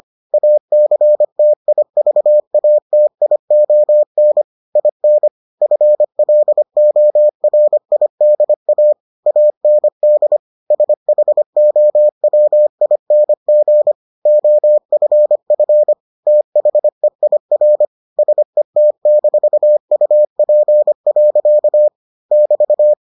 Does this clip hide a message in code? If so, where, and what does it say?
Yes: Morse code from 0.3 to 22.9 s, heard as a keyed tone, about 25 wpm, 600 Hz. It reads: ACTIVATION IN FLORIDA AND SHOWING OFF THEIR SET-UP. <BT>